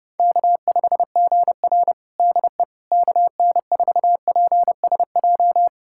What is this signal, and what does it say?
Morse code, heard as a keyed tone, about 29 wpm, 700 Hz. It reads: K5GR DE KN4PSJ
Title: USA Callsign DE Callsign 30wpm